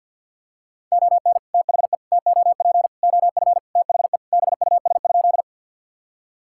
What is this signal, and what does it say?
Morse code, heard as a keyed tone, about 46 wpm, 700 Hz. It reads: ON THE TOP OF THE BUS?
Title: Sentences from Top 1000 Words - Review Part 6 - 50wpm